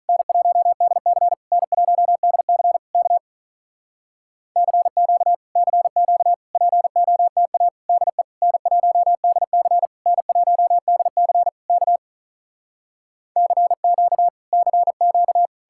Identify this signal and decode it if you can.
Morse code, heard as a keyed tone, about 46 wpm, 700 Hz. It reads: N1DC N1DC K CQ CQ POTA DE N1DC N1DC K CQ CQ